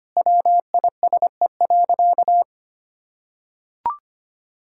Morse code, heard as a keyed tone, about 25 wpm, 700 Hz. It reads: WISE.
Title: Sentences from Top 1600 Words 25wpm